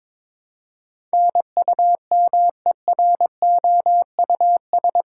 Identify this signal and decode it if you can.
Morse code, heard as a keyed tone, about 22 wpm, 700 Hz. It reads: NUMEROUS